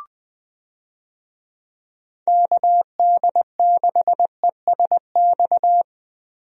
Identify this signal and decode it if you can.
Morse code, heard as a keyed tone, about 20 wpm, 700 Hz. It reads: KD6ESX
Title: USA Call Signs - 6 Characters - Repeated 2x - 20wpm